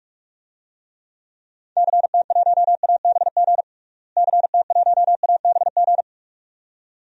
Morse code, heard as a keyed tone, about 42 wpm, 700 Hz. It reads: CT1ABG CT1ABG
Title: DXCC-100 Call Signs - 6 Characters - Repeated 2x - 45wpm